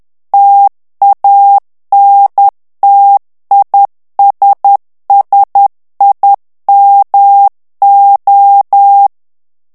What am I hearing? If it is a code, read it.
Morse code, heard as a keyed tone, about 10 wpm, 800 Hz. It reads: TANTISSIMO
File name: morse mp3 / ITALIEN texte clair 1 vitesse 10 mots minute